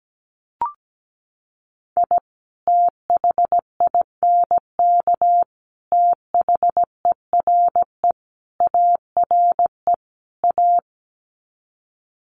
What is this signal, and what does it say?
Morse code, heard as a keyed tone, about 17 wpm, 700 Hz. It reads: I THINK THERE ARE A
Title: Sets of 5 Words Encore2 17wpm